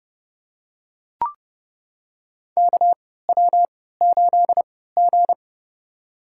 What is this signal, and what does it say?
Morse code, heard as a keyed tone, about 30 wpm, spaced as for 17 wpm, 700 Hz. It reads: KW8G
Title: USA Call Signs - 4 Characters - 3x Character Spacing - 30wpm